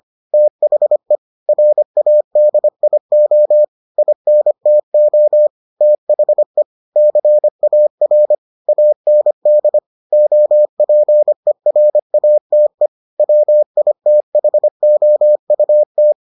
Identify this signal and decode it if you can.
Morse code, heard as a keyed tone, about 25 wpm, 600 Hz. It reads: THE RADIO INTO THE CAR AND OPERATE WITHOUT